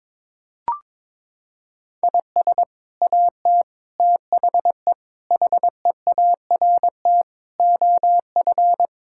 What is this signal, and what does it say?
Morse code, heard as a keyed tone, about 22 wpm, 700 Hz. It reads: IS AT THE HEART OF
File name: practice / 22wpm / Sets of 6 Words - 22wpm